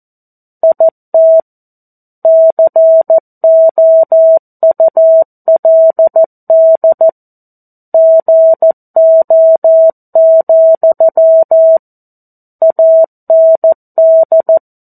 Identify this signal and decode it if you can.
Morse code, heard as a keyed tone, about 14 wpm, 650 Hz. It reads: IT COULD GO, AND
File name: 02-Chapter-02 14 WPM